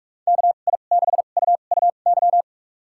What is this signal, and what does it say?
Morse code, heard as a keyed tone, about 44 wpm, spaced as for 32 wpm, 700 Hz. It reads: KI6UUY
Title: USA Call Signs - 6 Characters - 2x Character Spacing - 45wpm